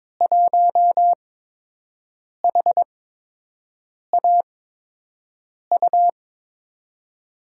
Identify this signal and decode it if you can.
Morse code, heard as a keyed tone, about 22 wpm, spaced as for 6 wpm, 700 Hz. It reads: 1HAU
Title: USA Call Signs - 8x Character Spacing - 22wpm